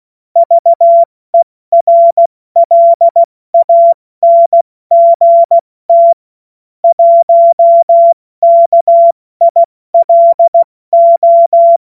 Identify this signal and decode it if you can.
Morse code, heard as a keyed tone, about 16 wpm, 680 Hz. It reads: VERLANGT 1 KILO
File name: Morsekurs / Witze / Witz 08. August 2023 70 F80